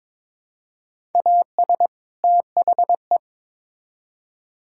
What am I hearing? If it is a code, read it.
Morse code, heard as a keyed tone, about 22 wpm, 700 Hz. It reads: AS THE
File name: Build A Phrase - 2-Word Phrases - 22wpm